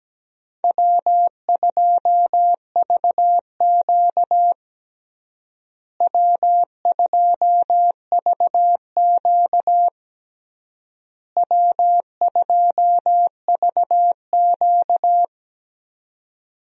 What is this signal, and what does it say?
Morse code, heard as a keyed tone, about 17 wpm, 700 Hz. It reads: W2VQ W2VQ W2VQ
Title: USA Call Signs - 4 Characters - Repeated 3x - 17wpm